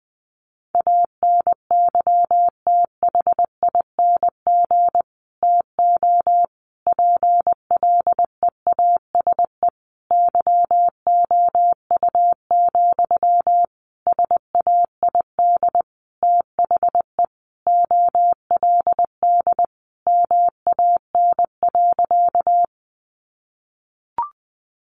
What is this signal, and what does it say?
Morse code, heard as a keyed tone, about 20 wpm, 700 Hz. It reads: ANYTHING TO PLEASE YOU, SAID THE OLD MAN.
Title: Sentences from Top 1500 Words 20wpm